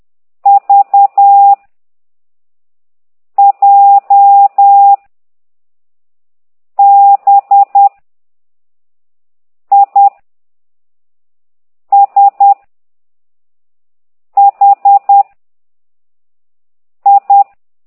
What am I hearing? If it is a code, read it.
Morse code, heard as a keyed tone, about 10 wpm, 800 Hz. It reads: V J B I S H I
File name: ct2idw / exerc6